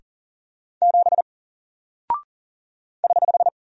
Morse code, heard as a keyed tone, about 40 wpm, 700 Hz. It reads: 7 <HH>
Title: Single Character (Letter:Number:Punctuation:Prosign) Alt 40wpm